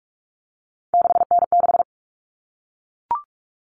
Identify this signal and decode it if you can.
Morse code, heard as a keyed tone, about 44 wpm, 700 Hz. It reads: 6N6
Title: All Cut Numbers - 3-Digit 45wpm